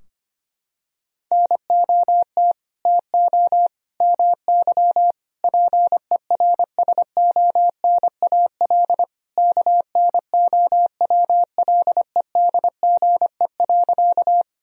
Morse code, heard as a keyed tone, about 25 wpm, 700 Hz. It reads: NOT TO MY PERSONAL KNOWLEDGE.
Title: Sentences from Top 1000 Words - Review Part 4 - 25wpm